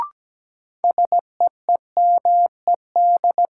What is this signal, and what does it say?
Morse code, heard as a keyed tone, about 17 wpm, 700 Hz. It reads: SEEMED
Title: Sets of 3 Words - New Word Drill - 17wpm